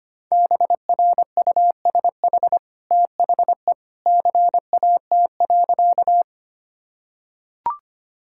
Morse code, heard as a keyed tone, about 25 wpm, 700 Hz. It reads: BRUSH THE CAT.
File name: Easy Sentences with Hints - 25wpm